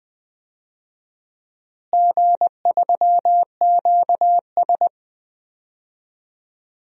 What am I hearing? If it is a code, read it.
Morse code, heard as a keyed tone, about 20 wpm, 700 Hz. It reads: G3QS
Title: DXCC-100 Call Signs - 4 Characters - Repeated 3x - 20wpm